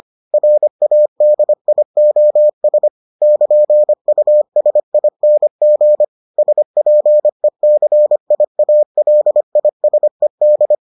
Morse code, heard as a keyed tone, about 25 wpm, 600 Hz. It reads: RADIOS <KN>USING SPECIALISED